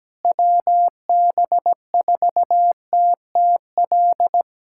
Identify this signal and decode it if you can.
Morse code, heard as a keyed tone, about 17 wpm, 700 Hz. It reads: WB4TTL
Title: USA Callsign DE Callsign - 6 Characters - 17wpm